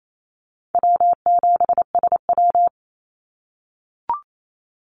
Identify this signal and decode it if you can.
Morse code, heard as a keyed tone, about 28 wpm, 700 Hz. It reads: W7SW
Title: Easy Call Signs - 28wpm